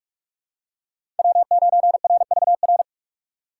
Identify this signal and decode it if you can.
Morse code, heard as a keyed tone, about 46 wpm, 700 Hz. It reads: W9RVR
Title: USA Call Signs - 5 Characters - 2x Character Spacing - 45wpm